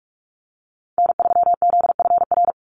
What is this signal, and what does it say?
Morse code, heard as a keyed tone, about 46 wpm, 690 Hz. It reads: N3ZFR
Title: USA Call Signs 45wpm